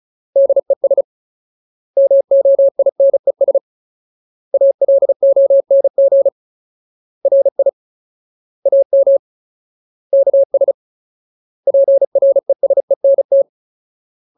Morse code, heard as a keyed tone, about 34 wpm, 550 Hz. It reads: DES MOINES ALONG RI AM KS PRESENT
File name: Ultimate Word List / 35WPM Ultimate